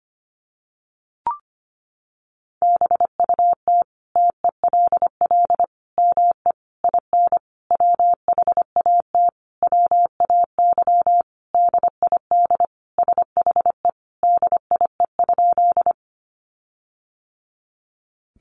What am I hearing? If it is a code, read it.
Morse code, heard as a keyed tone, about 25 wpm, 700 Hz. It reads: BUT TELL ME IN WHAT WAY DID SHE DIE?
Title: Sentences from Top 1300 Words 25wpm